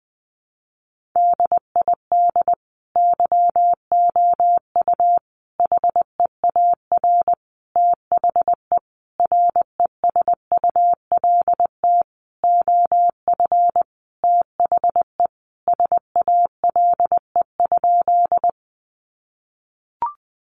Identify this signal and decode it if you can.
Morse code, heard as a keyed tone, about 20 wpm, 700 Hz. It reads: DID YOU HEAR THE RESULT OF THE SALE?
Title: Sentences from Top 1400 Words 20wpm